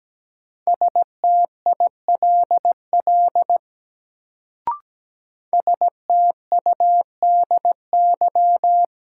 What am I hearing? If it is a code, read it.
Morse code, heard as a keyed tone, about 17 wpm, 700 Hz. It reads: STILL STUDY
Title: Sentences from Top 300 Words - New Word Drill - 17wpm